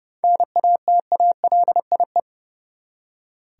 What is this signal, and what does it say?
Morse code, heard as a keyed tone, about 30 wpm, 700 Hz. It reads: NATALIE
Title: USA Common Names - 30wpm